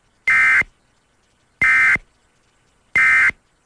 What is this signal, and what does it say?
Emergency Alert System (SAME) tones, heard as harsh buzzing bursts. It is heard from 0.2 to 3.3 s.